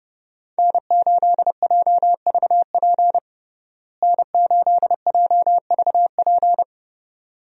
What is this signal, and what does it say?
Morse code, heard as a keyed tone, about 32 wpm, 700 Hz. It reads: N8JVP N8JVP
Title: USA Call Signs - 5 Characters - Repeated 2x - 30wpm